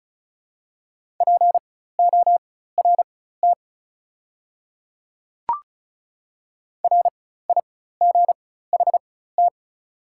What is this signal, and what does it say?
Morse code, heard as a keyed tone, about 36 wpm, spaced as for 16 wpm, 700 Hz. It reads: PORT RIGHT
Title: 4x Character Spacing - Top 300 Words - 35wpm